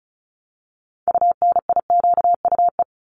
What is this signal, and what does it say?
Morse code, heard as a keyed tone, about 34 wpm, 700 Hz. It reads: UNIQUE
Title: Top 751-1000 Words 35wpm